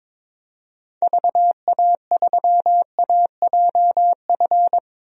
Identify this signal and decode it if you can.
Morse code, heard as a keyed tone, about 22 wpm, 700 Hz. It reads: VA3AJF
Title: DXCC-100 Call Signs - 6 Characters - Repeated 3x - 22wpm